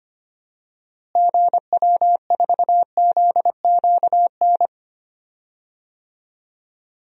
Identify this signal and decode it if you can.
Morse code, heard as a keyed tone, about 25 wpm, 700 Hz. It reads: GW4ZQN
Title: DXCC-100 Call Signs - 6 Characters - Repeated 3x - 25wpm